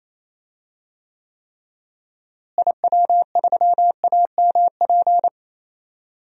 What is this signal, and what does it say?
Morse code, heard as a keyed tone, about 28 wpm, 700 Hz. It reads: IW3AMP
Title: DXCC-100 Call Signs - 6 Characters - Repeated 3x - 28wpm